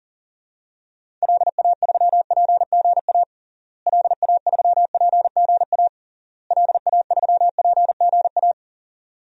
Morse code, heard as a keyed tone, about 37 wpm, 700 Hz. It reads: LA3PGA LA3PGA LA3PGA
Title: DXCC-100 Call Signs - 6 Characters - Repeated 3x - 40wpm